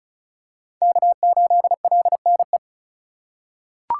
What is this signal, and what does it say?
Morse code, heard as a keyed tone, about 36 wpm, 700 Hz. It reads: K8LNE
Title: USA Call Signs - 5 Characters - 2x Character Spacing - 35wpm